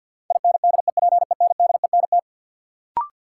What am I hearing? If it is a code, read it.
Morse code, heard as a keyed tone, about 53 wpm, 700 Hz. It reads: INDEPENDENT
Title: Sentences from Top 900 Words - New Word Drill - 50wpm